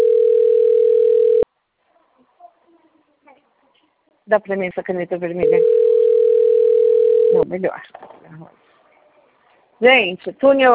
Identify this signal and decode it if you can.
Ringback tone (North America), 440 + 480 Hz, with speech.
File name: out- - - - - .38566